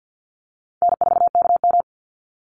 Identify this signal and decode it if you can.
Morse code, heard as a keyed tone, about 50 wpm, 700 Hz. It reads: N4KM
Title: Easy Call Signs - 50wpm